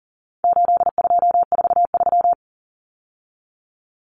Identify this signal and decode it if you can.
Morse code, heard as a keyed tone, about 40 wpm, 700 Hz. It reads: 8243
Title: Common Cut Numbers - 4-Digit 40wpm